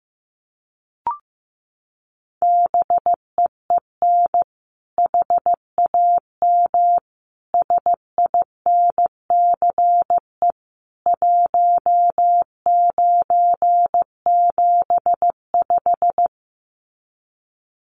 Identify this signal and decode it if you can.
Morse code, heard as a keyed tone, about 15 wpm, 700 Hz. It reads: BEEN HAM SINCE 1975
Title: Common QSO Phrases 15wpm